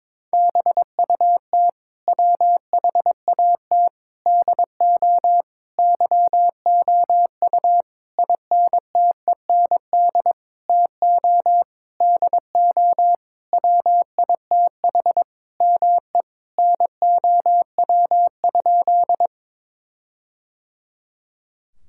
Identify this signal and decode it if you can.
Morse code, heard as a keyed tone, about 23 wpm, 700 Hz. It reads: BUT WHAT DO YOU INTEND TO DO WITH ME NOW?
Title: Sentences from Top 1600 Words 22wpm